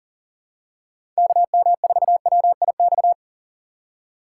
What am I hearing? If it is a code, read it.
Morse code, heard as a keyed tone, about 40 wpm, 700 Hz. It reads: KM4WIX
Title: USA Call Signs - 2x Character Spacing - 40wpm